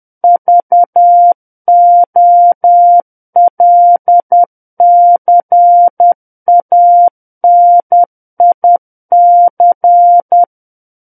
Morse code, heard as a keyed tone, about 10 wpm, 700 Hz. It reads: VOLCANIC
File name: Book One / 01-Chapter 1 - 10 WPM